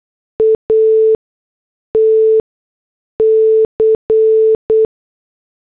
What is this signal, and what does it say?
Morse code, heard as a keyed tone, about 8 wpm, spaced as for 6 wpm, 440 Hz. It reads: ATC